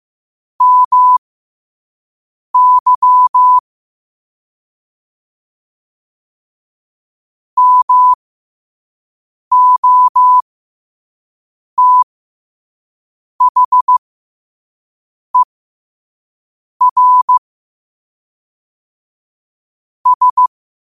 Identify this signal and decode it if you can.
Morse code, heard as a keyed tone, about 15 wpm, spaced as for 5 wpm, 1000 Hz. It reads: MY MOTHER S
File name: qotdfiles / 5wpm